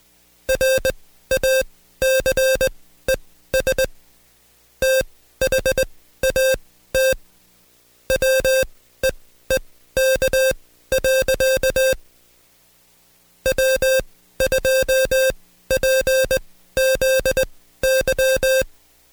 Morse code, heard as a keyed tone, about 20 wpm, spaced as for 13 wpm, 530 Hz. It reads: RACES THAT WEEK. W2PZY
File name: Code Quick / AudioDownld2 / Seg133-Stepper6 13WPM